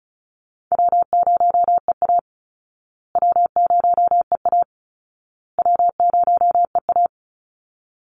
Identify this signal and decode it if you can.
Morse code, heard as a keyed tone, about 34 wpm, 700 Hz. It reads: W0EA W0EA W0EA
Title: Call Signs Repeated 2x - 35wpm